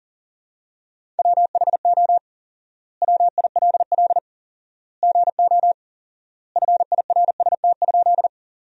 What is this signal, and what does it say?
Morse code, heard as a keyed tone, about 40 wpm, 700 Hz. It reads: WHO WILL GO FIRST?
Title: 4x Word-Spacing - Sentences from Top 100 Words - 40wpm